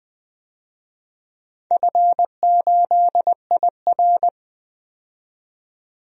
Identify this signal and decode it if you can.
Morse code, heard as a keyed tone, about 21 wpm, 700 Hz. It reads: F8IR
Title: DXCC-100 Call Signs - 4 Characters - 20wpm